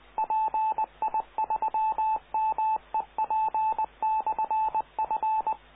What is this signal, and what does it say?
Morse code, heard as a keyed tone, about 20 wpm, 870 Hz, with radio noise. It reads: PI3MEP/F